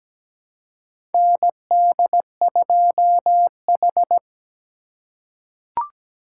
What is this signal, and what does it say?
Morse code, heard as a keyed tone, about 17 wpm, 700 Hz. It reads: ND2H
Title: USA Call Signs - 4 Characters - 3x Character Spacing - 17wpm